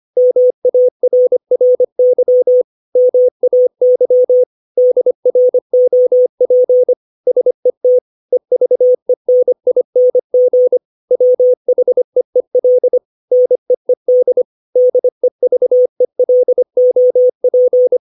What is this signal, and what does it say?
Morse code, heard as a keyed tone, about 25 wpm, 500 Hz. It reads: MARRY MAY DROP SET EVENING WHEEL NEED DEVELOP
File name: 632 words / 632 Words 25WPM